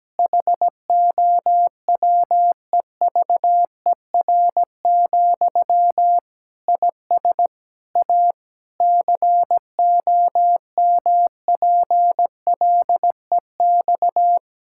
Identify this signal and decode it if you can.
Morse code, heard as a keyed tone, about 17 wpm, 700 Hz. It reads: HOWEVER, IS A COMPLEX
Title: Sentences from Top 1000 Words - Review Part 6 - 17wpm